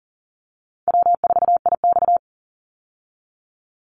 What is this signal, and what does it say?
Morse code, heard as a keyed tone, about 38 wpm, 700 Hz. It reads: W4IX
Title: Easy Call Signs Repeated 2x - 40wpm